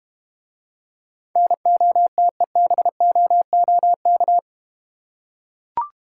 Morse code, heard as a keyed tone, about 31 wpm, 700 Hz. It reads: NOTEBOOK
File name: Easy Words - Speed-Racing with Hints - 25wpm